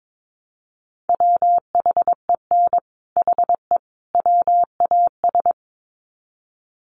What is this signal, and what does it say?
Morse code, heard as a keyed tone, about 22 wpm, 700 Hz. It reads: WHEN HE WAS
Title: Sets of 3 Words 22wpm